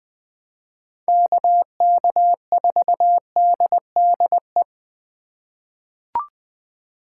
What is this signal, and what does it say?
Morse code, heard as a keyed tone, about 20 wpm, 700 Hz. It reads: KK4DDE
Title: USA Call Signs - 6 Characters - 3x Character Spacing - 20wpm